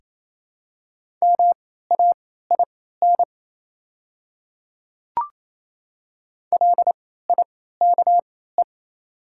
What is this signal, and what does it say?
Morse code, heard as a keyed tone, about 28 wpm, spaced as for 16 wpm, 690 Hz. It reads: MAIN LIKE
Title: 3x Character Spacing - Top 300 Words - 28wpm